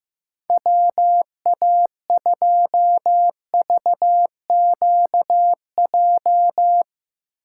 Morse code, heard as a keyed tone, about 16 wpm, 700 Hz. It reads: WA2VQJ
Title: USA Call Signs - 6 Characters - 4x Character Spacing - 15wpm